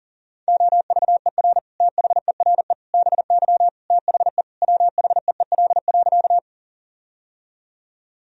Morse code, heard as a keyed tone, about 40 wpm, 700 Hz. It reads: OVER THERE BY THE WHEEL.
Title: Sentences from Top 1000 Words - Review Part 3 - 40wpm